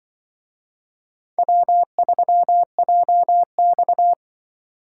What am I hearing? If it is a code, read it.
Morse code, heard as a keyed tone, about 25 wpm, 700 Hz. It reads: W3JX
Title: Speed Racing - USA 1x2 Call Signs 20wpm